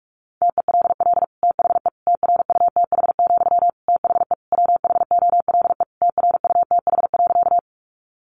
Morse code, heard as a keyed tone, about 44 wpm, 700 Hz. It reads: TELL THE TRUTH, THE WHOLE TRUTH.
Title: Sentences from Top 800 Words 45wpm